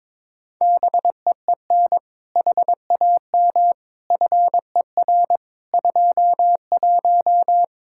Morse code, heard as a keyed tone, about 22 wpm, 700 Hz. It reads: BEEN HAM FER 21
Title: Practice QSO - Exchange 3 - More Info - 22wpm